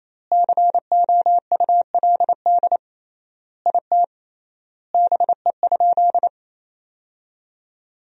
Morse code, heard as a keyed tone, about 27 wpm, 700 Hz. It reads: COULD IT BE?
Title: 3x Word-Spacing - Sentences from Top 100 Words - 28wpm